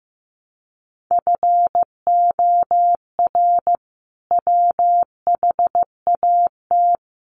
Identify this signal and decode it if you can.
Morse code, heard as a keyed tone, about 15 wpm, 700 Hz. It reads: FOR WHAT
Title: Sentences from Top 700 Words 15wpm